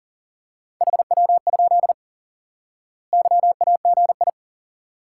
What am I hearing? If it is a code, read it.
Morse code, heard as a keyed tone, about 40 wpm, 700 Hz. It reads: HW? YAGI
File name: IQE Course 3 - Lesson 89 - YAGI - Two Elements - 40wpm